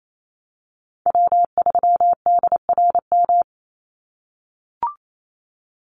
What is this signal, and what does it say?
Morse code, heard as a keyed tone, about 28 wpm, 700 Hz. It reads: W3DRM
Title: USA Call Signs 28wpm